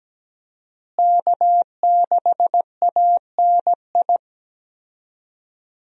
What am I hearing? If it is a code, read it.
Morse code, heard as a keyed tone, about 17 wpm, 700 Hz. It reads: K6ANI
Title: USA Call Signs - 5 Characters - Repeated 3x - 17wpm